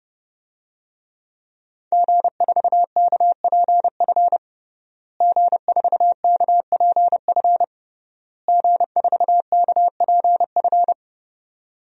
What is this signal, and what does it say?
Morse code, heard as a keyed tone, about 32 wpm, 700 Hz. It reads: G4KPF G4KPF G4KPF
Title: DXCC-100 Call Signs - 5 Characters - Repeated 3x - 30wpm